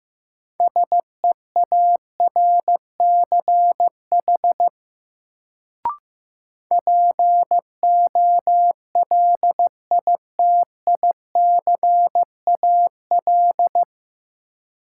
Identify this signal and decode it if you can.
Morse code, heard as a keyed tone, about 15 wpm, 700 Hz. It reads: SEARCH POLITICAL
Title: Sentences from Top 800 Words - New Word Drill - 15wpm